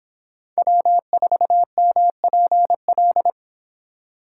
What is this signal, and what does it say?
Morse code, heard as a keyed tone, about 26 wpm, 700 Hz. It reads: W4MPL